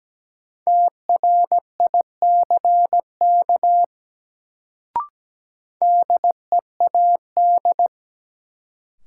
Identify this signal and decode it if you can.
Morse code, heard as a keyed tone, about 17 wpm, 700 Hz. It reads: TRICK DEAD
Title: Sentences from Top 1500 Words - New Word Drill - 17wpm